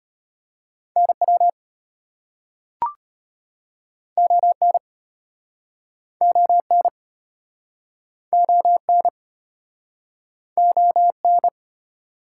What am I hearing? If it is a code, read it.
Morse code, heard as a keyed tone, about 33 wpm, 700 Hz. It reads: NW ON ON ON ON